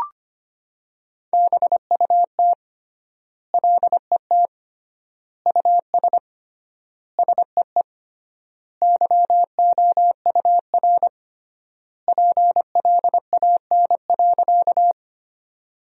Morse code, heard as a keyed tone, about 25 wpm, 700 Hz. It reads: BUT LET US SEE YOUR PLAN.
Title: 3x Word-Spacing - Sentences from Top 500 Words - 25wpm